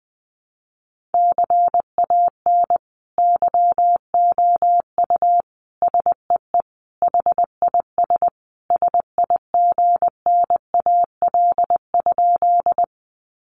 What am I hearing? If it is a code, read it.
Morse code, heard as a keyed tone, about 20 wpm, 700 Hz. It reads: CAN YOU SEE HIS SIGNAL?
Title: Sentences from Top 1400 Words 20wpm